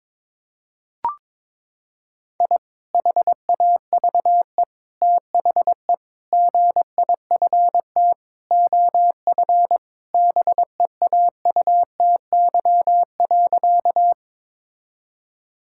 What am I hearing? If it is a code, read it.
Morse code, heard as a keyed tone, about 22 wpm, 700 Hz. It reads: I HAVE THE GIFT OF BEAUTY.
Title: Sentences from Top 1000 Words - Review Part 6 - 22wpm